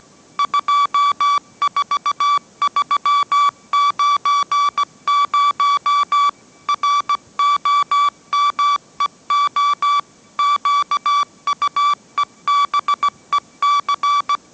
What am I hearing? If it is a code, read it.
Morse code, heard as a keyed tone, about 17 wpm, 1150 Hz. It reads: 24390 ROMEO QUEBEC
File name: mrx / mrx080 18wpm